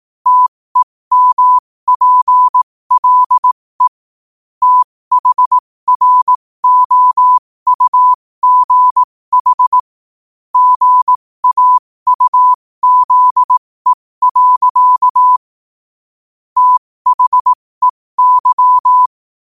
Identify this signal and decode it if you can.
Morse code, heard as a keyed tone, about 18 wpm, spaced as for 15 wpm, 1000 Hz. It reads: TEMPLE THROUGH GAUZE. THEY
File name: qotdfiles / 15wpm